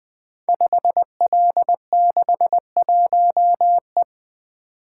Morse code, heard as a keyed tone, about 20 wpm, 700 Hz. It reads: 5L61E
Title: Code Groups - 5 Letter:Numbers - 20wpm